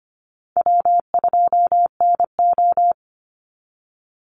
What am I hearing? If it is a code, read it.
Morse code, heard as a keyed tone, about 25 wpm, 700 Hz. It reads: W2NO
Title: Easy Call Signs - 25wpm